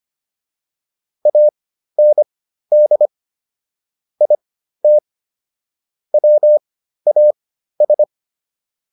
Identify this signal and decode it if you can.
Morse code, heard as a keyed tone, about 25 wpm, spaced as for 13 wpm, 600 Hz. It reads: AND IT WAS